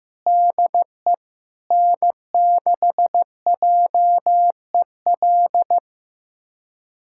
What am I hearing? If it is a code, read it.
Morse code, heard as a keyed tone, about 15 wpm, 700 Hz. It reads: DE N6JEL
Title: USA Callsign DE Callsign - 5 Characters - 15wpm